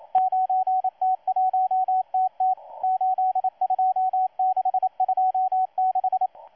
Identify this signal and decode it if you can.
Morse code, heard as a keyed tone, about 27 wpm, 750 Hz, with radio noise. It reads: 9T1TT 82626
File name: M12-2016-07-07-1310utc-13926khz-error-restart